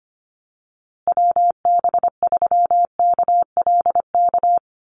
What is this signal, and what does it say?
Morse code, heard as a keyed tone, about 25 wpm, 700 Hz. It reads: WB3KLK
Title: USA Call Signs Repeated 3x 25wpm